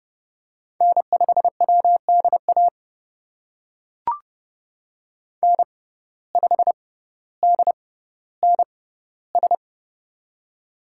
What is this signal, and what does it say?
Morse code, heard as a keyed tone, about 30 wpm, spaced as for 15 wpm, 700 Hz. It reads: N5WDA N5DNS
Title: USA Call Signs - 5 Characters - 6x Character Spacing - 30wpm